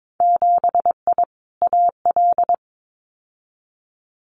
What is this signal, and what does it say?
Morse code, heard as a keyed tone, about 22 wpm, 700 Hz. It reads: 7I AL
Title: Winter Field Day Exchanges Encore 22wpm